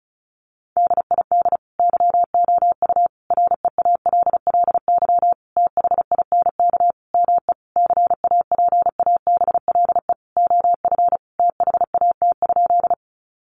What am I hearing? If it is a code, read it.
Morse code, heard as a keyed tone, about 34 wpm, 700 Hz. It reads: DID YOU REALLY THINK ME CAPABLE OF THAT?
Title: Sentences from Top 1200 Words 35wpm